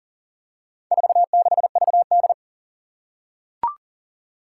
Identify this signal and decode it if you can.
Morse code, heard as a keyed tone, about 41 wpm, 700 Hz. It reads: 46VD
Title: Four Challenging Letter-Number Combinations 40wpm